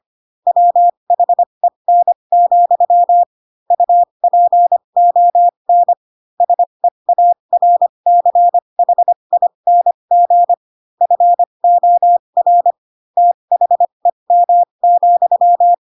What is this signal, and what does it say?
Morse code, heard as a keyed tone, about 25 wpm, spaced as for 22 wpm, 700 Hz. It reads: WHEN, UPON SEARCHING FOR THEM,
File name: Pinocchio / Pinocchio ch3 18 25 ik1hge